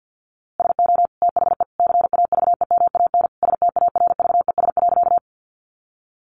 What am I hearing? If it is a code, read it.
Morse code, heard as a keyed tone, about 48 wpm, 700 Hz. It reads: SO THE CAVEMAN STARVES.
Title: A Sound Of Thunder 50wpm